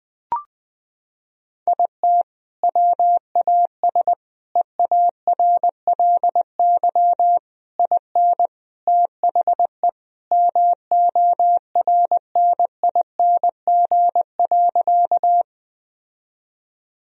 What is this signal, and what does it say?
Morse code, heard as a keyed tone, about 21 wpm, 700 Hz. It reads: IT WAS EARLY IN THE MORNING.
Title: Sentences from Top 1000 Words - Review Part 2 - 20wpm